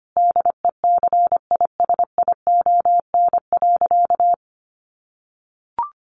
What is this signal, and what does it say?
Morse code, heard as a keyed tone, about 25 wpm, 700 Hz. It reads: DECISION.
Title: Sentences from Top 1200 Words 25wpm